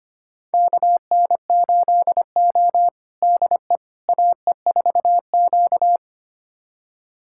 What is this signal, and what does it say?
Morse code, heard as a keyed tone, about 25 wpm, 700 Hz. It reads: KN8O DE AE4Q
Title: USA Callsign DE Callsign - 4 Characters - 25wpm